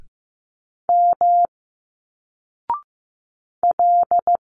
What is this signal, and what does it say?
Morse code, heard as a keyed tone, about 15 wpm, 700 Hz. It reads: M L
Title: Single Letters 15wpm